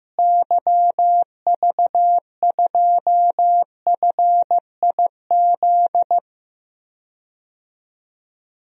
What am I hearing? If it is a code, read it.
Morse code, heard as a keyed tone, about 15 wpm, 700 Hz. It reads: YV2FIZ
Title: DXCC-100 Call Signs - 6 Characters - Repeated 2x - 15wpm